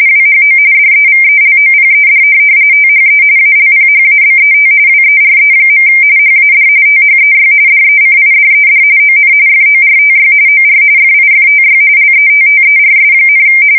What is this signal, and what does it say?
RTTY, about 45.45 baud. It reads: WHO DEVELOPED THE ALPHABET WERE. HAD THEY DECIDED ON FORTY OR FIFTY LETTERS,
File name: fsk170 / printing-telegraph